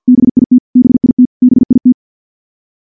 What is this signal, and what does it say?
Morse code, heard as a keyed tone, about 50 wpm, 250 Hz. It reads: BIT BIT BIT
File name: morse / morse-training-50WPM